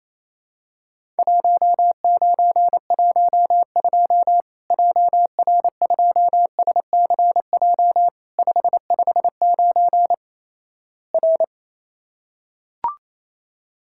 Morse code, heard as a keyed tone, about 28 wpm, 700 Hz. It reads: 1912 JR2SCJ 559 R
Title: Worked All Europe DX Contest - QTC Exchange - 28wpm